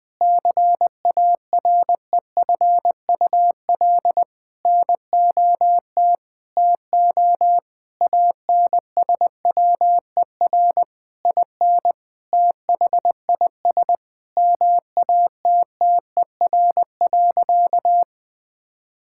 Morse code, heard as a keyed tone, about 20 wpm, 700 Hz. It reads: CAREFUL NOT TO ANSWER IN THIS MATTER.
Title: Sentences from Top 1600 Words 20wpm